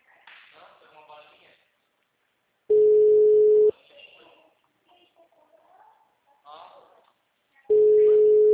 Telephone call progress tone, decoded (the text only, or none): ringback tone